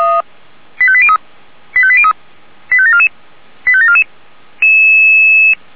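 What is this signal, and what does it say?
Five-tone selective call (ZVEI), heard as quick runs of five tones, tones of about 70 ms. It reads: 85803 85803 85744 85744
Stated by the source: ZVEI five-tone call received by SDR.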